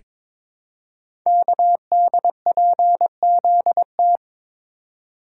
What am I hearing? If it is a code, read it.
Morse code, heard as a keyed tone, about 23 wpm, 700 Hz. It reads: KDPZT
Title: Code Groups - 5 Letters - 22wpm